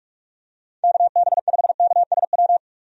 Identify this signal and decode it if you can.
Morse code, heard as a keyed tone, about 43 wpm, 700 Hz. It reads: KB5KSW
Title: USA Call Signs - 6 Characters - 45wpm